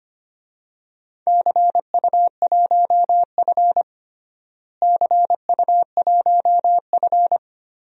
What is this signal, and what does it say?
Morse code, heard as a keyed tone, about 25 wpm, 700 Hz. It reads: CU1F CU1F
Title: DXCC-100 Call Signs - 4 Characters - Repeated 2x - 25wpm